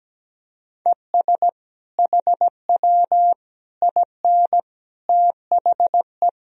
Morse code, heard as a keyed tone, about 17 wpm, 700 Hz. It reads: ES HW IN THE
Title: Sentences from Top 200 Words and Abbreviations 17wpm